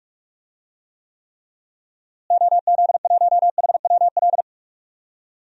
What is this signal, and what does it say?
Morse code, heard as a keyed tone, about 46 wpm, 700 Hz. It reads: OZ1HWL
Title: DXCC-100 Call Signs - 6 Characters - Repeated 3x - 45wpm